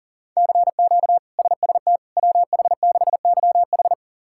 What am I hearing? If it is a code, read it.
Morse code, heard as a keyed tone, about 37 wpm, 700 Hz. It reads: CQ SST WH6YH
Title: K1USN SST QSO Practice - 40wpm